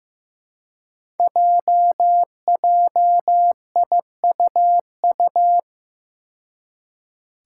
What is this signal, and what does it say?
Morse code, heard as a keyed tone, about 15 wpm, 700 Hz. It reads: JJIUU
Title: Five Challenging Letter-Number Combinations 15wpm